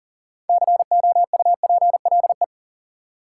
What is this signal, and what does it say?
Morse code, heard as a keyed tone, about 40 wpm, 700 Hz. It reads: COUPLE
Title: Sets of 2 Words - New Word Drill - 40wpm